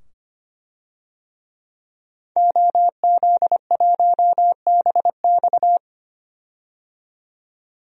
Morse code, heard as a keyed tone, about 25 wpm, 700 Hz. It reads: OZ1BX
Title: DXCC-100 Call Signs - 5 Characters - 25wpm